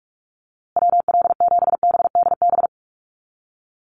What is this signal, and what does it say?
Morse code, heard as a keyed tone, about 44 wpm, 700 Hz. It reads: WL7BDB
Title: USA Call Signs Repeated 2x 45wpm